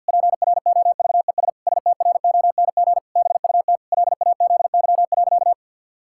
Morse code, heard as a keyed tone, about 50 wpm, 700 Hz. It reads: PROVES STRONG BUT LAZY.
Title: Pangram Challenge v2 - 50wpm